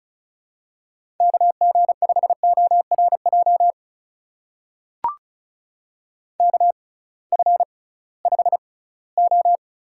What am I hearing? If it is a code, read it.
Morse code, heard as a keyed tone, about 36 wpm, spaced as for 20 wpm, 700 Hz. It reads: KG5ORJ KF5O
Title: USA Call Signs - 6 Characters - 6x Character Spacing - 35wpm